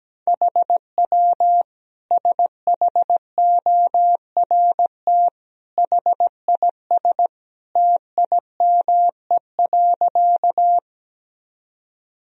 Morse code, heard as a keyed tone, about 17 wpm, 700 Hz. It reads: HW SHORT HIS TIME.
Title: Sentences from Top 400 Words and Abbreviations 17wpm